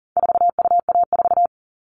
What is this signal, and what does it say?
Morse code, heard as a keyed tone, about 38 wpm, 700 Hz. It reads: 4UA4
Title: All Cut Numbers - 4-Digit 40wpm